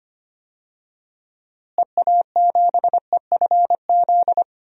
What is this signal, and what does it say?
Morse code, heard as a keyed tone, about 25 wpm, 700 Hz. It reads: EA7EFZ
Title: DXCC-100 Call Signs - 6 Characters - Repeated 2x - 25wpm